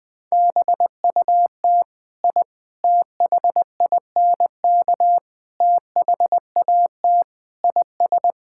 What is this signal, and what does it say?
Morse code, heard as a keyed tone, about 21 wpm, 700 Hz. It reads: BUT I THINK THAT IS
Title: Sentences with 4-Word Phrases - Spoken Beginning - 20wpm